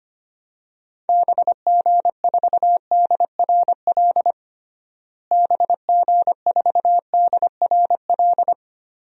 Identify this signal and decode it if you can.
Morse code, heard as a keyed tone, about 25 wpm, 700 Hz. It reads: BG4DRL BG4DRL
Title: DXCC-100 Call Signs - 6 Characters - Repeated 3x - 25wpm